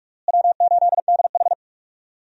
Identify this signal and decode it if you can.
Morse code, heard as a keyed tone, about 47 wpm, 700 Hz. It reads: W8DH
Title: USA Call Signs - 4 Characters - Repeated 2x - 45wpm